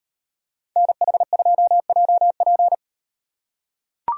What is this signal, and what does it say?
Morse code, heard as a keyed tone, about 38 wpm, 700 Hz. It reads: NH2JP